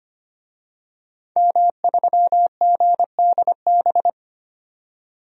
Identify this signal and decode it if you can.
Morse code, heard as a keyed tone, about 25 wpm, 700 Hz. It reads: M3GDB
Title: DXCC-100 Call Signs - 5 Characters - 25wpm